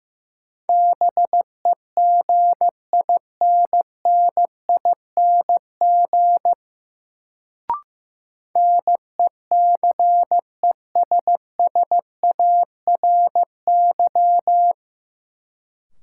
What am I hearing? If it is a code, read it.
Morse code, heard as a keyed tone, about 15 wpm, 700 Hz. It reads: BEGINNING NECESSARY
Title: Sentences from Top 800 Words - New Word Drill - 15wpm